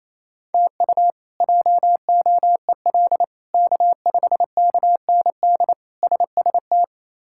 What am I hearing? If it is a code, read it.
Morse code, heard as a keyed tone, about 28 wpm, 700 Hz. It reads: TU JOEL K5KND SST
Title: K1USN SST QSO Practice - 28wpm